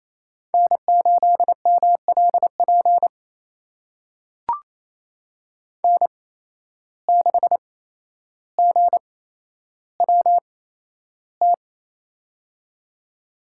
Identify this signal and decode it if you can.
Morse code, heard as a keyed tone, about 28 wpm, spaced as for 12 wpm, 700 Hz. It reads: N8MLP N6GWT
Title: USA Call Signs - 5 Characters - 8x Character Spacing - 28wpm